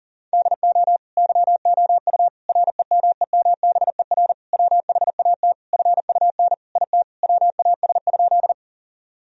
Morse code, heard as a keyed tone, about 40 wpm, 700 Hz. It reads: DO YOU REMEMBER WHAT FUN IT WAS?
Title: Sentences from Top 1000 Words - Review Part 4 - 40wpm